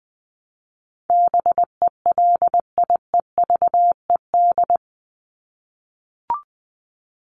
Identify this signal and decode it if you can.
Morse code, heard as a keyed tone, about 20 wpm, 700 Hz. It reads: BELIEVED
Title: ED Suffix - Top 100 Words with Base Words 20wpm